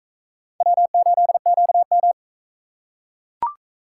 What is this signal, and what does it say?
Morse code, heard as a keyed tone, about 40 wpm, 700 Hz. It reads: W8QM